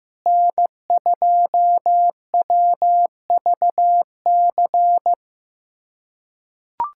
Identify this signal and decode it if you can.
Morse code, heard as a keyed tone, about 15 wpm, 700 Hz. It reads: N2WVC
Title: USA Callsign DE Callsign 15wpm